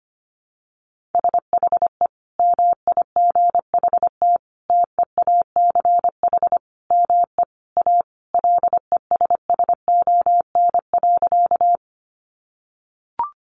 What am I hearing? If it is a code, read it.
Morse code, heard as a keyed tone, about 25 wpm, 700 Hz. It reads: SHE MIGHT TEACH ME A LESSON.
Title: Sentences from Top 1200 Words 25wpm